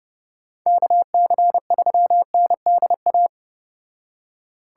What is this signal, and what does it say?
Morse code, heard as a keyed tone, about 31 wpm, 700 Hz. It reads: KC3NDA